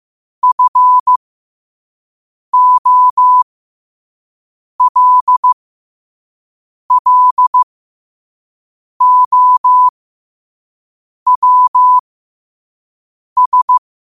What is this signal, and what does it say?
Morse code, heard as a keyed tone, about 15 wpm, spaced as for 5 wpm, 1000 Hz. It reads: FOLLOWS
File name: qotdfiles / 5wpm